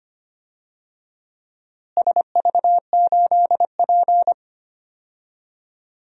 Morse code, heard as a keyed tone, about 25 wpm, 700 Hz. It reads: SV8P
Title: DXCC-100 Call Signs - 4 Characters - Repeated 2x - 25wpm